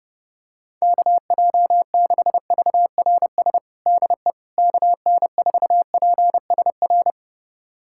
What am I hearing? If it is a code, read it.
Morse code, heard as a keyed tone, about 30 wpm, 700 Hz. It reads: KJ6VRS DE KN4PSR
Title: USA Callsign DE Callsign - 6 Characters - 30wpm